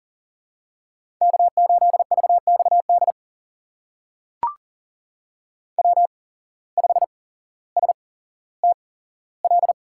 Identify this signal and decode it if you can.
Morse code, heard as a keyed tone, about 38 wpm, spaced as for 17 wpm, 700 Hz. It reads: K8VXD W5STL
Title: USA Call Signs - 5 Characters - 8x Character Spacing - 40wpm